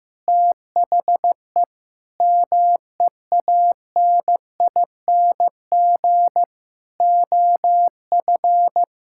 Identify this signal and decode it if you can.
Morse code, heard as a keyed tone, about 15 wpm, 700 Hz. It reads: THE MEANING OF
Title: Sentences from Top 1000 Words - Review Part 6 - 15wpm